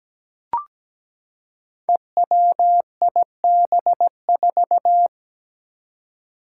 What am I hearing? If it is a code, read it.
Morse code, heard as a keyed tone, about 17 wpm, 700 Hz. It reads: EWIB4
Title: Five Challenging Letter-Number Combinations 17wpm